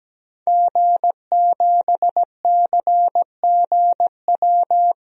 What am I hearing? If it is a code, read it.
Morse code, heard as a keyed tone, about 17 wpm, 700 Hz. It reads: G7CGW
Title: DXCC-100 Call Signs - 5 Characters - 17wpm